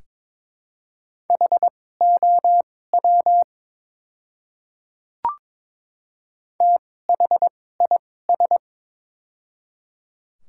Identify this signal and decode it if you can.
Morse code, heard as a keyed tone, about 22 wpm, spaced as for 16 wpm, 700 Hz. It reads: HOW THIS
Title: 2x Character Spacing - Top 300 Words - 22wpm